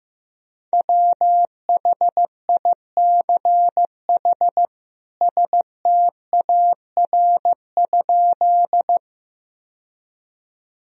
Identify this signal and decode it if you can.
Morse code, heard as a keyed tone, about 15 wpm, 700 Hz. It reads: WHICH STAR?
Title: Sentences from Top 1000 Words - Review Part 2 - 15wpm